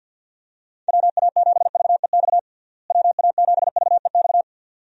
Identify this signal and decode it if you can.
Morse code, heard as a keyed tone, about 51 wpm, 700 Hz. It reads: WA7VEX WA7VEX
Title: USA Call Signs - 6 Characters - Repeated 2x - 50wpm